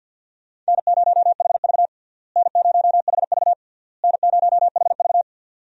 Morse code, heard as a keyed tone, about 50 wpm, 700 Hz. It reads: N0HV N0HV N0HV
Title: USA Call Signs - 4 Characters - Repeated 3x - 50wpm